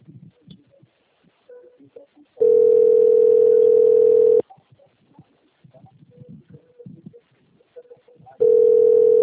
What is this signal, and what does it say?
Ringback tone (North America), 440 + 480 Hz.